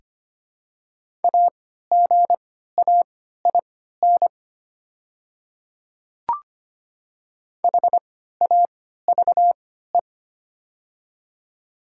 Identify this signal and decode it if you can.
Morse code, heard as a keyed tone, about 25 wpm, spaced as for 14 wpm, 700 Hz. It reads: AGAIN HAVE
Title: 3x Character Spacing - Top 300 Words - 25wpm